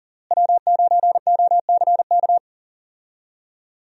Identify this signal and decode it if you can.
Morse code, heard as a keyed tone, about 40 wpm, 700 Hz. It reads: W9OCK
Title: USA Call Signs - 5 Characters - Repeated 2x - 40wpm